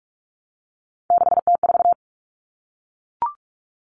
Morse code, heard as a keyed tone, about 44 wpm, 700 Hz. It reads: 6T4
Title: Common Cut Numbers - 3-Digit 45wpm